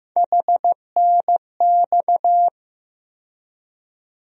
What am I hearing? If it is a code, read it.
Morse code, heard as a keyed tone, about 15 wpm, 700 Hz. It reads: HNX
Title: USA Call Sign Postfixes - 3 Letters 15wpm